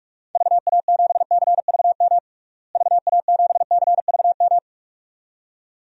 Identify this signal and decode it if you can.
Morse code, heard as a keyed tone, about 46 wpm, 700 Hz. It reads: VA7CVM VA7CVM
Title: DXCC-100 Call Signs - 6 Characters - Repeated 2x - 45wpm